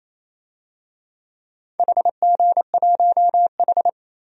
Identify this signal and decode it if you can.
Morse code, heard as a keyed tone, about 28 wpm, 700 Hz. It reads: HG1H
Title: DXCC-100 Call Signs - 4 Characters - 28wpm